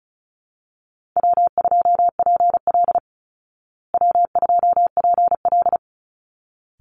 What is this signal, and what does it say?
Morse code, heard as a keyed tone, about 34 wpm, 700 Hz. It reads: W2PL W2PL
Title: Call Signs Repeated 3x - 35wpm